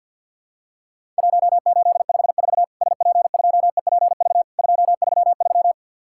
Morse code, heard as a keyed tone, about 50 wpm, 700 Hz. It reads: 1854 SP2EPV 233
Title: Worked All Europe DX Contest - QTC Exchange - 50wpm